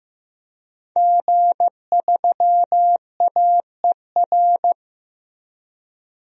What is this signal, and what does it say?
Morse code, heard as a keyed tone, about 15 wpm, 700 Hz. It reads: G3AER
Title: DXCC-100 Call Signs - 5 Characters - Repeated 3x - 15wpm